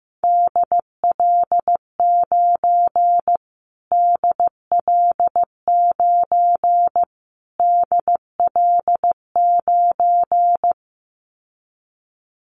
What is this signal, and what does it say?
Morse code, heard as a keyed tone, about 15 wpm, 700 Hz. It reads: DL9 DL9 DL9
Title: Callsign Prefixes for Top 25 DXCC - Repeated 3x - 15wpm